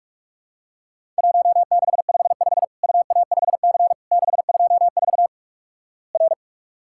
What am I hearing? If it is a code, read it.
Morse code, heard as a keyed tone, about 46 wpm, 700 Hz. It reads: 1655 UA5C 624 R
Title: Worked All Europe DX Contest - QTC Exchange - 45wpm